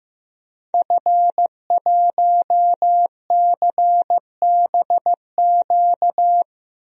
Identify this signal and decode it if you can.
Morse code, heard as a keyed tone, about 16 wpm, 700 Hz. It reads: F1CBQ
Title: DXCC-100 Call Signs - 5 Characters - 15wpm